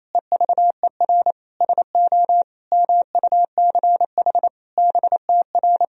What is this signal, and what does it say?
Morse code, heard as a keyed tone, about 28 wpm, 700 Hz. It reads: EVER SO MUCH BTR
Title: Sentences from Top 400 Words and Abbreviations 28wpm